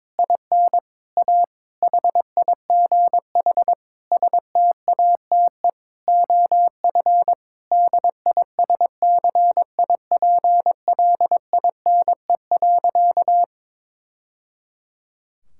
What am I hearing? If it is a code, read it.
Morse code, heard as a keyed tone, about 22 wpm, 700 Hz. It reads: IN A HIGH STATE OF DISCIPLINE.
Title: Sentences from Top 1000 Words - Review Part 6 - 22wpm